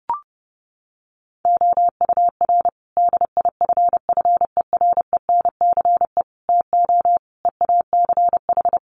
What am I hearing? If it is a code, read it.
Morse code, heard as a keyed tone, about 30 wpm, 700 Hz. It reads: OUR DIFFERENCE TO EACH
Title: Sentences from Top 1500 Words 30wpm